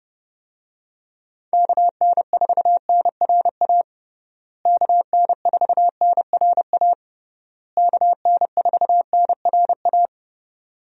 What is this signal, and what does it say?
Morse code, heard as a keyed tone, about 30 wpm, 700 Hz. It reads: KN4NRA KN4NRA KN4NRA
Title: USA Call Signs - 6 Characters - Repeated 3x - 30wpm